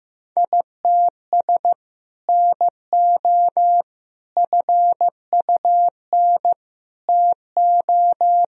Morse code, heard as a keyed tone, about 15 wpm, 700 Hz. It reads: ITS NO FUN TO
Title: Sentences from Top 1000 Words - Review Part 4 - 15wpm